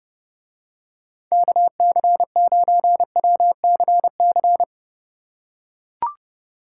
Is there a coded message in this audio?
Yes: Morse code from 1.3 to 4.6 s, heard as a keyed tone, about 28 wpm, 700 Hz. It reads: KC9WCC